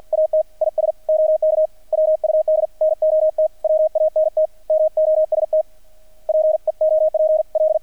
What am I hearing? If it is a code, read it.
Morse code, heard as a keyed tone, about 48 wpm, 630 Hz. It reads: AT IS OK WUD NOT WANT MOST PEOPL
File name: archived / CRSnet